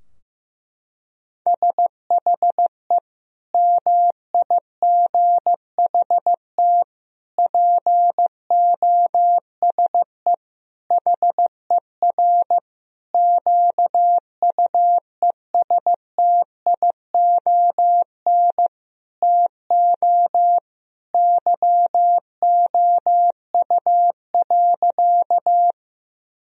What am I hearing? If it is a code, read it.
Morse code, heard as a keyed tone, about 15 wpm, 700 Hz. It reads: SHE MIGHT POSE HER QUESTION TO YOU.
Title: Sentences from Top 1000 Words - Review Part 2 - 15wpm